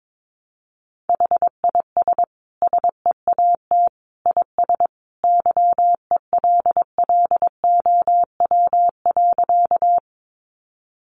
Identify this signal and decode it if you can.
Morse code, heard as a keyed tone, about 22 wpm, 700 Hz. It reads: HIS SEAT IS YELLOW.
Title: Additional Easy Phrases 22wpm